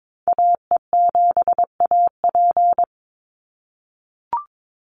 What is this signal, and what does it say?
Morse code, heard as a keyed tone, about 22 wpm, 700 Hz. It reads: AE7AP
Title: NA SOTA Activator Callsigns 22wpm